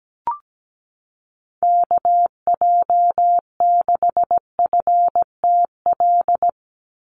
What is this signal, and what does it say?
Morse code, heard as a keyed tone, about 17 wpm, 700 Hz. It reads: KJ6FTL
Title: USA Call Signs 17wpm